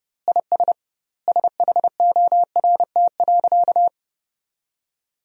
Morse code, heard as a keyed tone, about 30 wpm, 700 Hz. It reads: IS SHORT.
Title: 2x Word-Spacing - Sentences from Top 500 Words - 30wpm